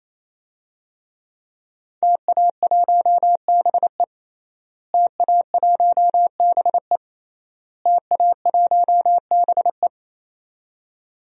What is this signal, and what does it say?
Morse code, heard as a keyed tone, about 28 wpm, 700 Hz. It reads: TA1BE TA1BE TA1BE
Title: DXCC-100 Call Signs - 5 Characters - Repeated 3x - 28wpm